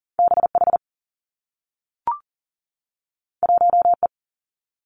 Morse code, heard as a keyed tone, about 39 wpm, 700 Hz. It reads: B H 1 E
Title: Two Challenging Letter-Number Combinations 40wpm